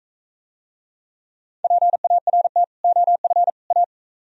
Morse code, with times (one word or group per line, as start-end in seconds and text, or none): 1.6-2.6 PART
2.8-3.5 OF
3.7-3.8 A